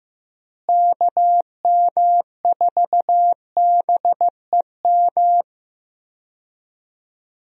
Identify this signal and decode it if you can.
Morse code, heard as a keyed tone, about 16 wpm, 700 Hz. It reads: KM4BEM
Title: USA Call Signs - 6 Characters - Repeated 2x - 15wpm